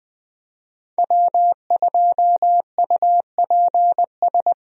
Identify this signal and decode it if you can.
Morse code, heard as a keyed tone, about 20 wpm, 700 Hz. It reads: W2UPS
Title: USA Call Signs - 5 Characters - 20wpm